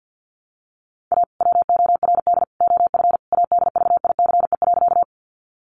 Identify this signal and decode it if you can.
Morse code, heard as a keyed tone, about 50 wpm, 700 Hz. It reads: A WORD OF ADVICE.
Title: Sentences from Top 1100 Words 50wpm